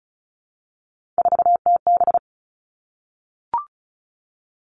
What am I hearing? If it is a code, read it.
Morse code, heard as a keyed tone, about 34 wpm, 700 Hz. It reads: 4TB
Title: All Cut Numbers - 3-Digit 35wpm